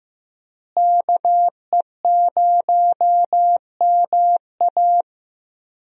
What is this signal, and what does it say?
Morse code, heard as a keyed tone, about 15 wpm, 700 Hz. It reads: KE0MA
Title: USA Call Signs - 5 Characters - Repeated 3x - 15wpm